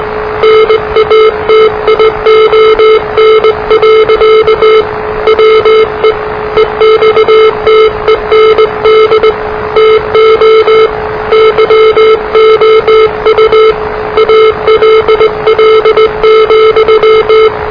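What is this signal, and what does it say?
Morse code, heard as a keyed tone, about 20 wpm, 440 Hz, with noise. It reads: NATION. WE EXTEND TO YOU ALL,
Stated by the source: Morse challenge from NZ Net, the national amateur radio net of Aotearoa/New Zealand, which meets on 3535.0 kHz.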